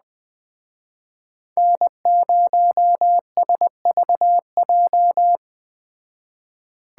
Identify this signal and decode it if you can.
Morse code, heard as a keyed tone, about 20 wpm, 700 Hz. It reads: N0SVJ
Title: Speed Racing - USA 1x3 Call Signs 17wpm